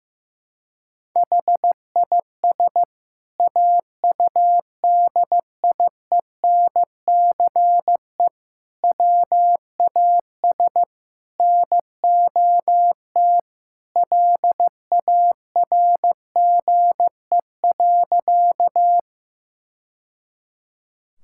Morse code, heard as a keyed tone, about 15 wpm, 700 Hz. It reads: HIS AUDIENCE WAS NOT LARGE.
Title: Sentences from Top 1000 Words - Review Part 5 - 15wpm